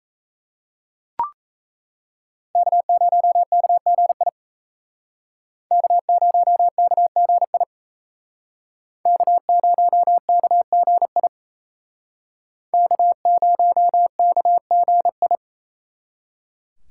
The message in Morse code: K0KGI K0KGI K0KGI K0KGI

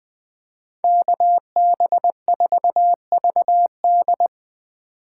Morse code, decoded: KB4VD

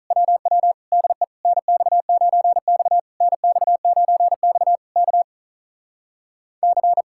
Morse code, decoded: WW DE NX9X NX9X K C

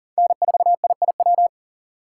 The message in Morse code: N4IIW

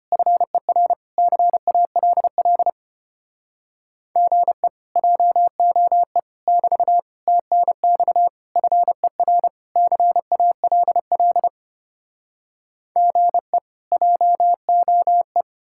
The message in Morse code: FER CALL GE JOE <BT> TNX FER CALL GE JOE

